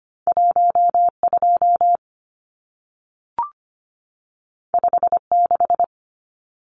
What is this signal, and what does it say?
Morse code, heard as a keyed tone, about 25 wpm, 700 Hz. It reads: 12 56
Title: Numbers - 2-Digit 25wpm